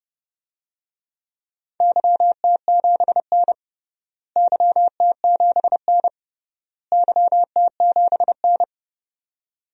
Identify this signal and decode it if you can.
Morse code, heard as a keyed tone, about 30 wpm, 700 Hz. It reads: YT7N YT7N YT7N
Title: DXCC-100 Call Signs - 4 Characters - Repeated 3x - 30wpm